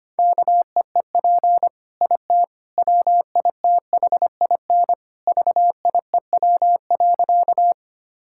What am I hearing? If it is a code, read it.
Morse code, heard as a keyed tone, about 25 wpm, 700 Hz. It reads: KEEP IT WITHIN VIEW.
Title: Sentences from Top 1000 Words - Review Part 6 - 25wpm